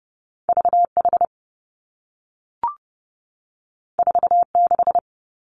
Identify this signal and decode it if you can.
Morse code, heard as a keyed tone, about 30 wpm, 700 Hz. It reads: V H 4 6
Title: Two Challenging Letter-Number Combinations 30wpm